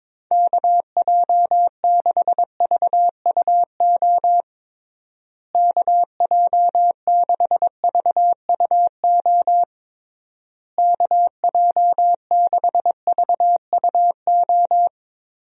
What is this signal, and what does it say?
Morse code, heard as a keyed tone, about 22 wpm, 700 Hz. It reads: KJ6VUO KJ6VUO KJ6VUO
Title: USA Call Signs - 6 Characters - Repeated 3x - 22wpm